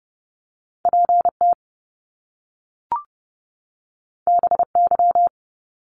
Morse code, heard as a keyed tone, about 30 wpm, 700 Hz. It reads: PT BY
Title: Top Two Letter Combinations in Words 30wpm